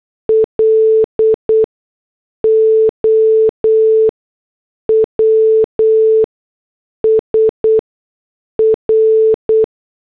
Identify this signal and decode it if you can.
Morse code, heard as a keyed tone, about 8 wpm, spaced as for 6 wpm, 440 Hz. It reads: LOWS R
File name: morse / headlines